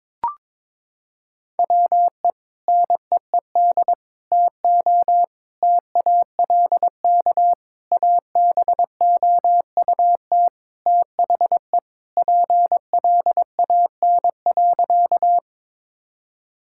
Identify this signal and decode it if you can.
Morse code, heard as a keyed tone, about 21 wpm, 700 Hz. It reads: WE NEED TO TALK ABOUT THE PLAN.
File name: practice / 22wpm / Sentences with 4-Word Phrases - 22wpm